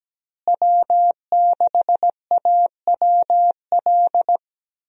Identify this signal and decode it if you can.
Morse code, heard as a keyed tone, about 17 wpm, 700 Hz. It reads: W6AWL
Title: USA Call Signs - 5 Characters - Repeated 2x - 17wpm